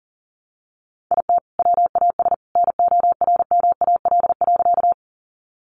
Morse code, heard as a keyed tone, about 40 wpm, 700 Hz. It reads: IT WAS NORMAL.
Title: Sentences from Top 1200 Words 40wpm